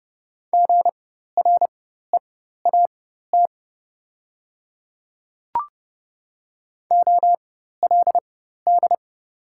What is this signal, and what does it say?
Morse code, heard as a keyed tone, about 30 wpm, spaced as for 14 wpm, 690 Hz. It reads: GREAT OLD
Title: 4x Character Spacing - Top 200 Words - 30wpm